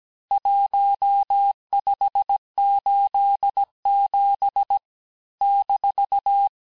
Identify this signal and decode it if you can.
Morse code, heard as a keyed tone, about 17 wpm, 800 Hz. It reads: 1587 -
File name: cours-du-lundi files / cours-1587-texte-ponctuations-vitesse-16mpm-puis-21mpm